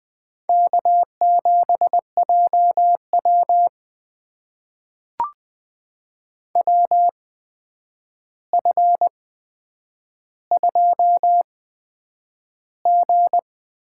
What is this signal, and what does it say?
Morse code, heard as a keyed tone, about 20 wpm, spaced as for 9 wpm, 700 Hz. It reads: K7JW WF2G
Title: USA Call Signs - 4 Characters - 8x Character Spacing - 20wpm